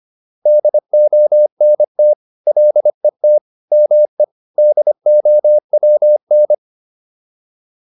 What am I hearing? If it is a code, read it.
Morse code, heard as a keyed tone, about 25 wpm, 600 Hz. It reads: DONT LET ME DOWN